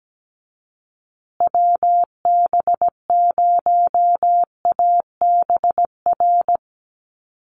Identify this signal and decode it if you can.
Morse code, heard as a keyed tone, about 17 wpm, 700 Hz. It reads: WB0ABR
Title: USA Call Signs Repeated 3x 17wpm